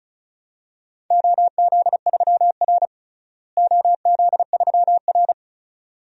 Morse code, heard as a keyed tone, about 36 wpm, 700 Hz. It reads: OZ3R OZ3R
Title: DXCC-100 Call Signs - 4 Characters - Repeated 2x - 35wpm